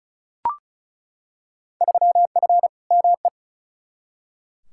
Morse code, heard as a keyed tone, about 34 wpm, 700 Hz. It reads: 3F ME
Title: Field Day Exchanges - 35wpm